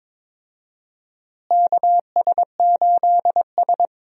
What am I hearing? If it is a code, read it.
Morse code, heard as a keyed tone, about 22 wpm, 700 Hz. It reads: KS8S
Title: USA Call Signs - 4 Characters - Repeated 3x - 22wpm